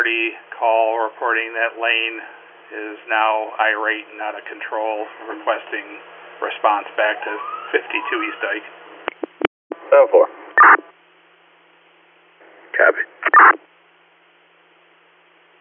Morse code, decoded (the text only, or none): none